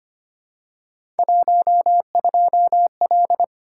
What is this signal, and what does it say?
Morse code, heard as a keyed tone, about 25 wpm, 700 Hz. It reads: 12L
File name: Three Letter-Number Combinations - 25wpm